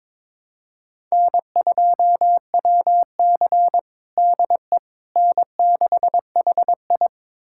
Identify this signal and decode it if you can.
Morse code, heard as a keyed tone, about 22 wpm, 700 Hz. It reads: N2WC DE N6HI
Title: USA Callsign DE Callsign - 4 Characters - 22wpm